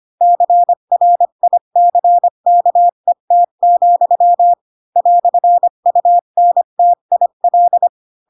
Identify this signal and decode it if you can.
Morse code, heard as a keyed tone, about 25 wpm, 700 Hz. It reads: CRICKET, "UNTIL
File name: Pinocchio / Pinocchio ch4 20 25 ik1hge